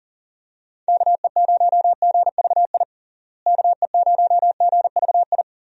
Morse code, heard as a keyed tone, about 40 wpm, 700 Hz. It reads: KE0GVI KE0GVI
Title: USA Call Signs - 6 Characters - Repeated 2x - 40wpm